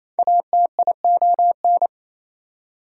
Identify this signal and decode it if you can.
Morse code, heard as a keyed tone, about 28 wpm, 700 Hz. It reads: ATION
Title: Prefixes and Suffixes 28wpm